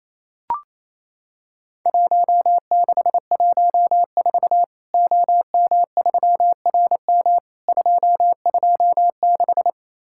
Morse code, heard as a keyed tone, about 28 wpm, 700 Hz. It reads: 1614 OM3RM 226
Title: Worked All Europe DX Contest - QTC Exchange - 28wpm